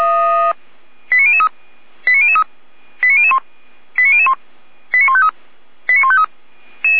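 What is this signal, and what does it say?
Five-tone selective call (ZVEI), heard as quick runs of five tones, tones of about 70 ms. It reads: 89003 89003 89001 89001 89153 89153
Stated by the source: ZVEI five-tone selective call received by SDR, from the feuerwehr-orlamuende.org audio archive.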